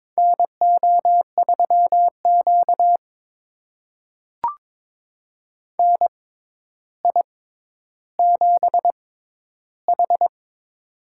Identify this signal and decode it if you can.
Morse code, heard as a keyed tone, about 22 wpm, spaced as for 11 wpm, 700 Hz. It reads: NO3Q NI7H
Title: USA Call Signs - 4 Characters - 6x Character Spacing - 22wpm